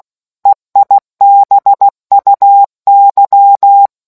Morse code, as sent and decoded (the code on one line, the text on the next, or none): . .. -... ..- -.--
EIBUY